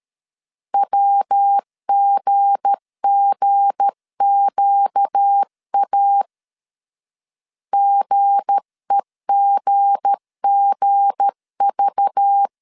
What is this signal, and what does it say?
Morse code, heard as a keyed tone, about 13 wpm, 800 Hz. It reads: WGGQA GEGGV
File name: LEZIONI / 7 GF9Z / 7 G 12wpm 2055